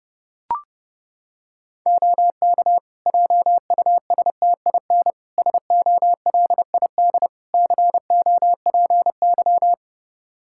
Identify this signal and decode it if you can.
Morse code, heard as a keyed tone, about 32 wpm, 700 Hz. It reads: OK JUSTIN SOLID COPY
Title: Practice QSO - Exchange 3 - More Info - 30wpm